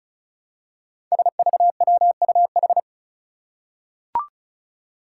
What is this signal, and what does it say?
Morse code, heard as a keyed tone, about 36 wpm, 700 Hz. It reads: SVWUH
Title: Five Challenging Letter-Number Combinations 35wpm